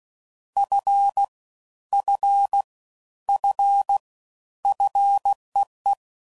Morse code, heard as a keyed tone, about 16 wpm, 800 Hz. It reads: F F F FEE
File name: Prefixes / FLs & THs / Single Fs Ls